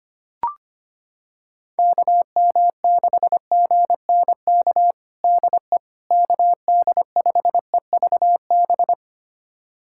Morse code, heard as a keyed tone, about 25 wpm, 700 Hz. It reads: KM6GNK DE KD5EVB
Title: USA Callsign DE Callsign - 6 Characters - 25wpm